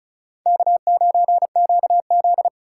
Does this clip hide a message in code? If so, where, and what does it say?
Yes: Morse code from 0.5 to 2.5 s, heard as a keyed tone, about 38 wpm, 700 Hz. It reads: K9QZ